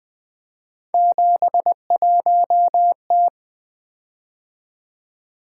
Morse code, heard as a keyed tone, about 20 wpm, 700 Hz. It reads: 71T
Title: Three Letter-Number Combinations - 20wpm